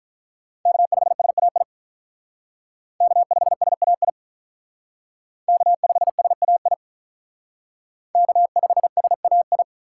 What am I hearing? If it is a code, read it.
Morse code, heard as a keyed tone, about 43 wpm, 700 Hz. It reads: K5SAI K5SAI K5SAI K5SAI
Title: Speed Racing - USA 1x3 Call Signs 35wpm